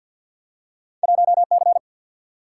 Morse code, heard as a keyed tone, about 47 wpm, 700 Hz. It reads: 1C
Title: Single-Letter Single-Number 50wpm